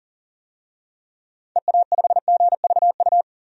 Morse code, heard as a keyed tone, about 40 wpm, 700 Hz. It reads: EA5GVU
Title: DXCC-100 Call Signs - 6 Characters - 40wpm